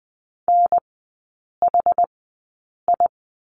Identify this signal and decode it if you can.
Morse code, heard as a keyed tone, about 20 wpm, spaced as for 8 wpm, 700 Hz. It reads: NHI
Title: 060 Lesson 11e - Letter h - Three Characters 20wpm